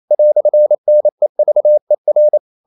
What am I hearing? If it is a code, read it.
Morse code, heard as a keyed tone, about 28 wpm, 600 Hz. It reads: "NEVER
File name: CW / Practice / EnglishQuotations0128